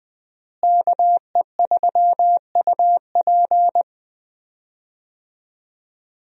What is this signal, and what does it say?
Morse code, heard as a keyed tone, about 20 wpm, 700 Hz. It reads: KE3UP
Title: USA Call Signs - 5 Characters - Repeated 3x - 20wpm